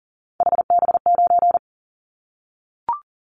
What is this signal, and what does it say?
Morse code, heard as a keyed tone, about 40 wpm, 700 Hz. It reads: HB9
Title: Callsign Prefixes for Top 25 DXCC - 40wpm